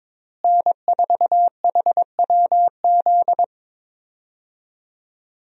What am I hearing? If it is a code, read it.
Morse code, heard as a keyed tone, about 21 wpm, 700 Hz. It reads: N4HWZ
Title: USA Call Signs - 5 Characters - Repeated 2x - 22wpm